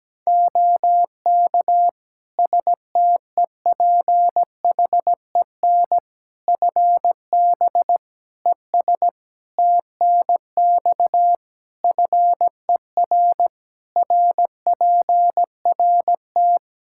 Morse code, heard as a keyed tone, about 17 wpm, 700 Hz. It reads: OK STEPHEN FB ES TNX FER RPRT
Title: Practice QSO - Exchange 2 - Info - 17wpm